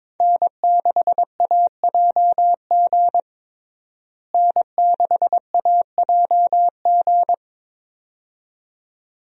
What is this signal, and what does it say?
Morse code, heard as a keyed tone, about 22 wpm, 700 Hz. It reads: N6AJG N6AJG
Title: USA Call Signs - 5 Characters - Repeated 2x - 22wpm